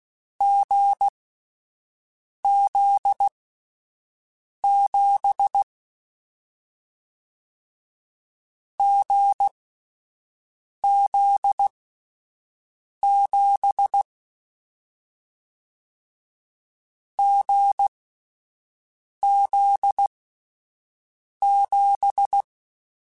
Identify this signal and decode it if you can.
Morse code, heard as a keyed tone, about 16 wpm, spaced as for 5 wpm, 800 Hz. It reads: GZ7 GZ7 GZ7
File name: Difficult Combos / Difficult Combos-mp3s / GZ GZ7 @5